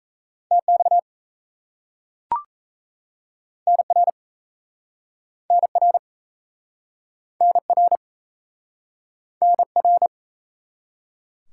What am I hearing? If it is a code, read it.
Morse code, heard as a keyed tone, about 36 wpm, 700 Hz. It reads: TX NR NR NR NR